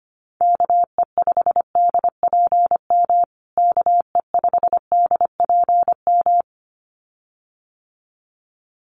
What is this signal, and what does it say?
Morse code, heard as a keyed tone, about 25 wpm, 700 Hz. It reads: KE5DPM KE5DPM
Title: USA Call Signs Repeated 2x 25wpm